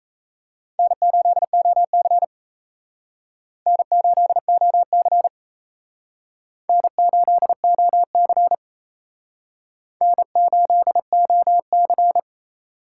Morse code, heard as a keyed tone, about 36 wpm, 700 Hz. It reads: N8OC N8OC N8OC N8OC